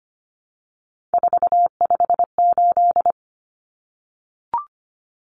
Morse code, heard as a keyed tone, about 25 wpm, 700 Hz. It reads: 458
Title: Common Cut Numbers - 3-Digit 25wpm